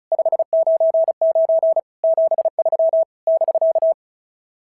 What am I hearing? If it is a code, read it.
Morse code, heard as a keyed tone, about 34 wpm, 650 Hz. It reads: 599 73 <BK>
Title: SOTA QSO Practice 35wpm